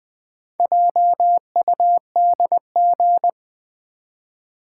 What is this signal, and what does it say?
Morse code, heard as a keyed tone, about 20 wpm, 700 Hz. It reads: JUDG
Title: Build A Word - 1 Letter At A Time - 5-Letter Words - 20wpm